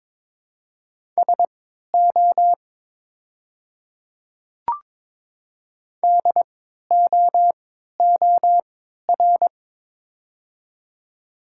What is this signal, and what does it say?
Morse code, heard as a keyed tone, about 22 wpm, spaced as for 12 wpm, 700 Hz. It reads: SO DOOR
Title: 3x Character Spacing - Top 300 Words - 22wpm